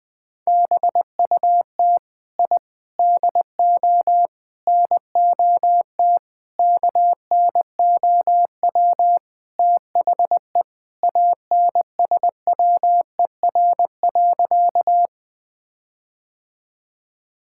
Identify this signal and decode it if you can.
Morse code, heard as a keyed tone, about 20 wpm, 700 Hz. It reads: BUT I DO NOT KNOW THE ANSWER.
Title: Sentences with 5-Word Phrases - Spoken Beginning - 20wpm